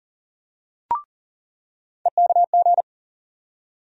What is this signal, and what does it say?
Morse code, heard as a keyed tone, about 40 wpm, 700 Hz. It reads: EKG
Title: USA Call Sign Postfixes - 3 Letters 40wpm